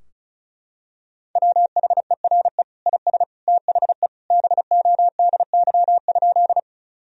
Morse code, heard as a keyed tone, about 34 wpm, 700 Hz. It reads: WHERE IS THE BODY?
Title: Sentences from Top 1000 Words - Review Part 2 - 35wpm